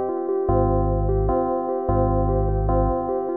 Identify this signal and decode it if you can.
There is no code.